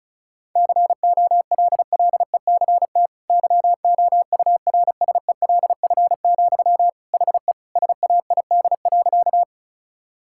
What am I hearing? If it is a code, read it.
Morse code, heard as a keyed tone, about 36 wpm, 700 Hz. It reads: COLLECT YOURSELF, HE SAID.
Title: Sentences from Top 1600 Words 35wpm